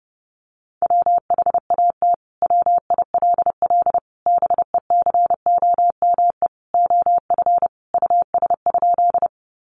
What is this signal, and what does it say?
Morse code, heard as a keyed tone, about 30 wpm, 700 Hz. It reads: WHAT WILL BECOME OF US?
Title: Sentences from Top 600 Words 30wpm